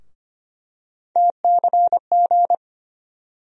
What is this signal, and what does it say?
Morse code, heard as a keyed tone, about 25 wpm, 700 Hz. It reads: TCG
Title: USA Call Sign Postfixes - 3 Letters 25wpm